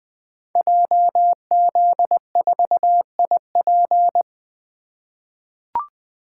Morse code, heard as a keyed tone, about 20 wpm, 700 Hz. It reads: JZ4IP
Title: Code Groups - 5 Letter:Numbers - 20wpm